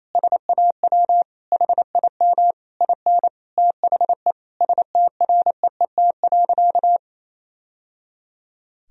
Morse code, heard as a keyed tone, about 28 wpm, 700 Hz. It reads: SAW HIM IN THE STREET.
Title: Sentences from Top 1000 Words - Review Part 3 - 28wpm